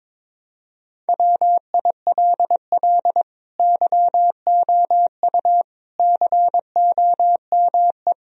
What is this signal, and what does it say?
Morse code, heard as a keyed tone, about 22 wpm, 700 Hz. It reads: WILL YOU COME
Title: Sentences from Top 1000 Words - Review Part 4 - 22wpm